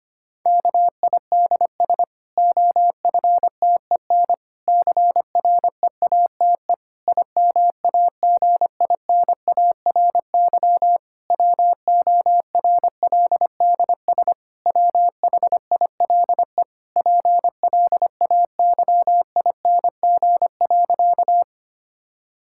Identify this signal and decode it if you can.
Morse code, heard as a keyed tone, about 25 wpm, 700 Hz. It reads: KIDS OFTEN CREATE IMAGINARY WORLDS WHILE PLAYING.
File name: practice / 25wpm / Difficult Sentences with Hints - 25wpm